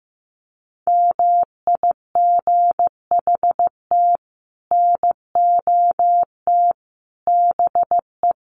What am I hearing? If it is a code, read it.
Morse code, heard as a keyed tone, about 15 wpm, 700 Hz. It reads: MIGHT NOT BE
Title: Sets of 5 Words Encore2 15wpm